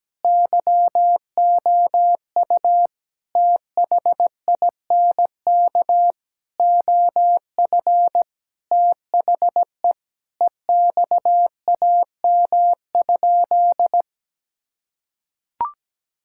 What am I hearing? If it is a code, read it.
Morse code, heard as a keyed tone, about 17 wpm, 700 Hz. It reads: YOU THINK OF THE EXAM?
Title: Sentences from Top 1000 Words - Review Part 5 - 17wpm